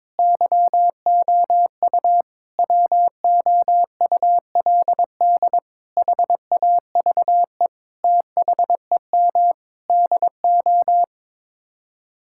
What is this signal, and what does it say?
Morse code, heard as a keyed tone, about 22 wpm, 700 Hz. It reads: YOU WOULD HAVE THEM DO
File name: Sets of 9 Words - 22wpm